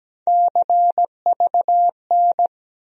Morse code, heard as a keyed tone, about 17 wpm, 700 Hz. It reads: CVN
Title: USA Call Sign Postfixes - 3 Letters 17wpm